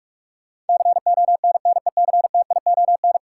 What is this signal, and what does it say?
Morse code, heard as a keyed tone, about 46 wpm, 700 Hz. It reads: CONNECTION
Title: Sentences from Top 1300 Words - New Word Drill - 45wpm